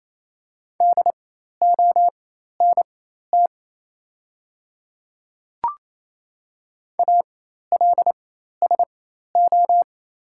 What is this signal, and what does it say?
Morse code, heard as a keyed tone, about 29 wpm, spaced as for 13 wpm, 700 Hz. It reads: DONT ALSO
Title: 4x Character Spacing - Top 500 Words - 28wpm